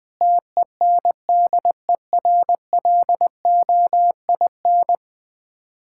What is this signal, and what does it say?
Morse code, heard as a keyed tone, about 20 wpm, 690 Hz. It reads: TENDERLOIN
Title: Difficult Words with Hints - 20wpm